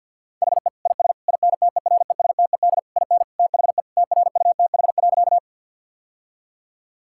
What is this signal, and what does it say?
Morse code, heard as a keyed tone, about 50 wpm, 700 Hz. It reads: HE IS INTERESTED IN THE TRUTH.
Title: Sentences from Top 1000 Words - Review Part 6 - 50wpm